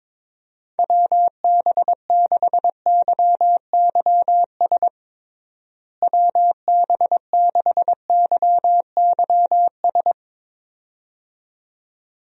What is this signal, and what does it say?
Morse code, heard as a keyed tone, about 22 wpm, 700 Hz. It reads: WB6YYS WB6YYS
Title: USA Call Signs - 6 Characters - Repeated 2x - 22wpm